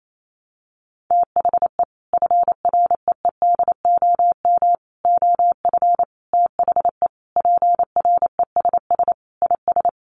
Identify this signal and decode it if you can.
Morse code, heard as a keyed tone, about 28 wpm, 700 Hz. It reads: THE FREEDOM OF THE PRESS IS
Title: Sentences from Top 800 Words 28wpm